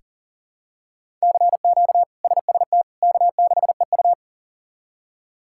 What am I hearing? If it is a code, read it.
Morse code, heard as a keyed tone, about 40 wpm, 700 Hz. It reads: CQ SST K6EU
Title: K1USN SST QSO Practice - 40wpm